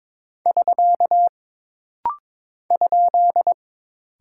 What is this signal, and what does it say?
Morse code, heard as a keyed tone, about 22 wpm, 700 Hz. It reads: <SK> ?
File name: practice / 22wpm / Single Punctuation-Prosign - Rapid-Fire - 22wpm